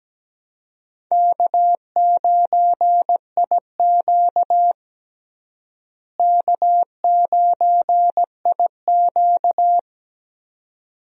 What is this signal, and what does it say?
Morse code, heard as a keyed tone, about 17 wpm, 700 Hz. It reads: K9IQ K9IQ
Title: USA Call Signs - 4 Characters - Repeated 2x - 17wpm